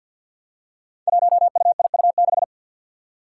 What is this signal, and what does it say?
Morse code, heard as a keyed tone, about 51 wpm, 700 Hz. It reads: 1UIU6
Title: Five Challenging Letter-Number Combinations 50wpm